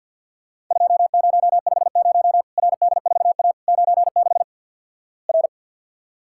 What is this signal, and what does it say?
Morse code, heard as a keyed tone, about 49 wpm, 700 Hz. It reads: 2050 RD4A 96 R
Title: Worked All Europe DX Contest - QTC Exchange with Repeats - 50wpm